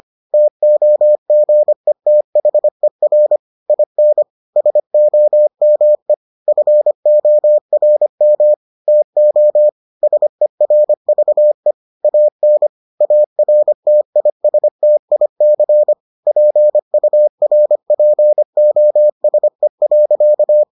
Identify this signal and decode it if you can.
Morse code, heard as a keyed tone, about 25 wpm, 600 Hz. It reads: TOGETHER IN SOME FORM TO SERVE AN ARTISTIC PURPOSE.